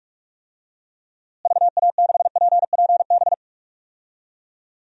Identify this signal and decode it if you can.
Morse code, heard as a keyed tone, about 44 wpm, 700 Hz. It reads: VA6PPB
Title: DXCC-100 Call Signs - 6 Characters - Repeated 2x - 45wpm